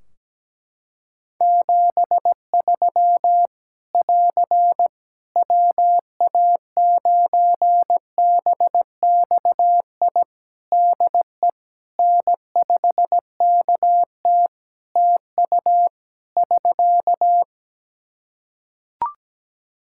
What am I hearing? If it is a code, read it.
Morse code, heard as a keyed tone, about 17 wpm, 700 Hz. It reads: 73 <AR> WA9BXI DE N5KT TU <SK>
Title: Practice QSO - Exchange 4 - Ending - 17wpm